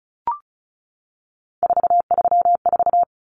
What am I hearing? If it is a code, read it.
Morse code, heard as a keyed tone, about 34 wpm, 700 Hz. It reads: 434
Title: Common Cut Numbers - 3-Digit 35wpm